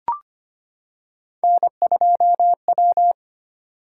Morse code, heard as a keyed tone, about 25 wpm, 700 Hz. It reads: N2W
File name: Three Letter-Number Combinations - 25wpm